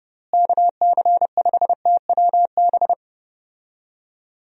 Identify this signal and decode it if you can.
Morse code, heard as a keyed tone, about 30 wpm, 700 Hz. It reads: KC5TWB
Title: USA Call Signs - 6 Characters - Repeated 2x - 30wpm